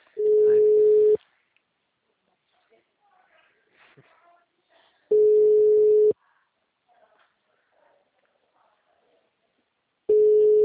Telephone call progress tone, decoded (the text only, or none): ringback tone